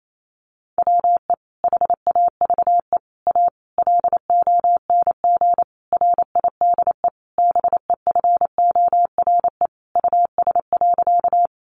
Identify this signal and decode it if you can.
Morse code, heard as a keyed tone, about 28 wpm, 700 Hz. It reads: WE HAVE A LONG RIDE BEFORE US.
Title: Sentences from Top 1200 Words 28wpm